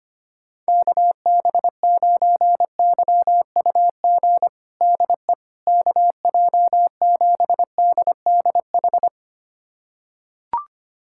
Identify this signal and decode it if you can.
Morse code, heard as a keyed tone, about 25 wpm, 700 Hz. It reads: KB9YUG DE KJ7DDH
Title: USA Callsign DE Callsign - 6 Characters - 25wpm